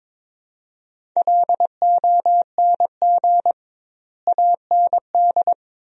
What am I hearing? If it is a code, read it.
Morse code, heard as a keyed tone, about 22 wpm, 700 Hz. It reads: LONG AND
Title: 2x Word-Spacing - Sentences from Top 400 Words - 22wpm